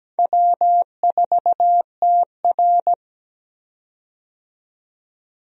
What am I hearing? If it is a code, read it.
Morse code, heard as a keyed tone, about 17 wpm, 700 Hz. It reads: W4TR
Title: USA Call Signs - 4 Characters - 17wpm